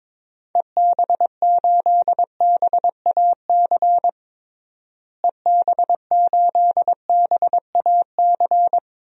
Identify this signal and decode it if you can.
Morse code, heard as a keyed tone, about 22 wpm, 700 Hz. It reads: EB8BAC EB8BAC
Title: DXCC-100 Call Signs - 6 Characters - Repeated 2x - 22wpm